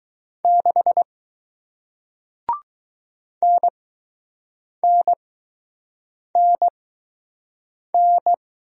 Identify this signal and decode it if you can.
Morse code, heard as a keyed tone, about 20 wpm, spaced as for 6 wpm, 700 Hz. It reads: 6 NNNN